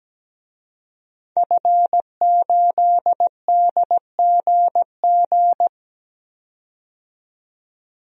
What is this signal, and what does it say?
Morse code, heard as a keyed tone, about 17 wpm, 700 Hz. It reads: F8DGG
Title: DXCC-100 Call Signs - 5 Characters - 17wpm